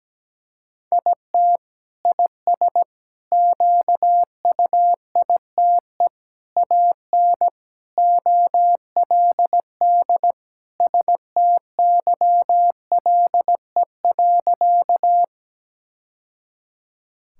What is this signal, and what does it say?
Morse code, heard as a keyed tone, about 17 wpm, 700 Hz. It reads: IT IS QUITE AN OLD STYLE.
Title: Sentences from Top 1000 Words - Review Part 5 - 17wpm